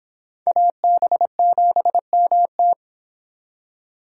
Morse code, heard as a keyed tone, about 26 wpm, 700 Hz. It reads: AB7MT